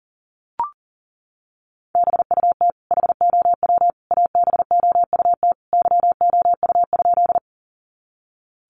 Morse code, heard as a keyed tone, about 40 wpm, 700 Hz. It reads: BUT HOW ABOUT YOU?
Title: Sentences from Top 100 Words 40wpm